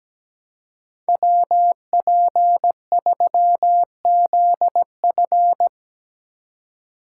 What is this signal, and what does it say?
Morse code, heard as a keyed tone, about 17 wpm, 700 Hz. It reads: WP3ZF
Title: USA Call Signs - 5 Characters - 2x Character Spacing - 17wpm